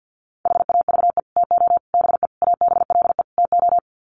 Morse code, heard as a keyed tone, about 50 wpm, 700 Hz. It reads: HAVE TO BE ABLE TO
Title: Sets of 5 Words 50wpm